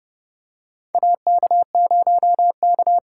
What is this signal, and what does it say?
Morse code, heard as a keyed tone, about 30 wpm, 700 Hz. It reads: AK0K
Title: USA Call Signs - 4 Characters - 3x Character Spacing - 30wpm